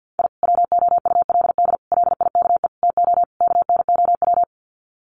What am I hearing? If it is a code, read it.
Morse code, heard as a keyed tone, about 50 wpm, 700 Hz. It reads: I WOULD LIKE TO KNOW
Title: Sets of 5 Words Encore 50wpm